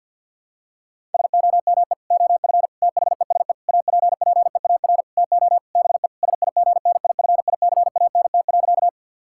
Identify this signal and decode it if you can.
Morse code, heard as a keyed tone, about 48 wpm, 700 Hz. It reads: SOME OF THESE APPEAR TO BE SIGNIFICANT.
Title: Sentences from Top 1000 Words - Review Part 6 - 50wpm